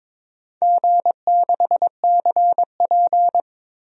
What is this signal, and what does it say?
Morse code, heard as a keyed tone, about 22 wpm, 700 Hz. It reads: G6CP
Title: DXCC-100 Call Signs - 4 Characters - 22wpm